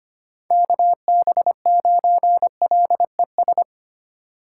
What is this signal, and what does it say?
Morse code, heard as a keyed tone, about 25 wpm, 700 Hz. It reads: KB9LES
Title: USA Call Signs - 6 Characters - 3x Character Spacing - 25wpm